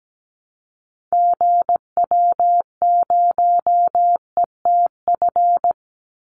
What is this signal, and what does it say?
Morse code, heard as a keyed tone, about 17 wpm, 700 Hz. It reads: GW0ETF
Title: Call Signs - 17wpm